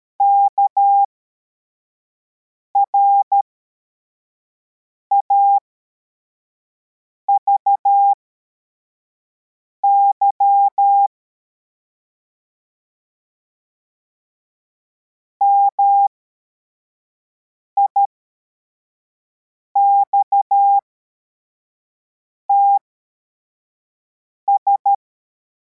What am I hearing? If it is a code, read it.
Morse code, heard as a keyed tone, about 13 wpm, spaced as for 4 wpm, 800 Hz. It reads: KRAVY MIXTS